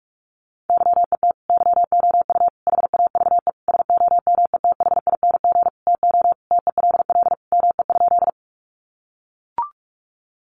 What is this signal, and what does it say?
Morse code, heard as a keyed tone, about 44 wpm, 700 Hz. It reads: YET YOU HAVE SOMETHING TO TELL ME?
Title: Sentences from Top 600 Words 45wpm